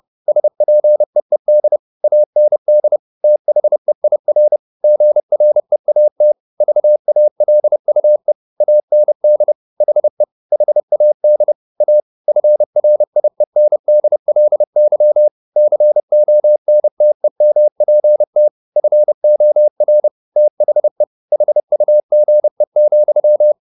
Morse code, heard as a keyed tone, about 30 wpm, 600 Hz. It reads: SPEED AND THEIR GREAT VALUE AND HE HAD A FRIENDLY CONTEMPT FOR THE HUGE,
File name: the old man and the sea / the old man and the sea 1 30wpm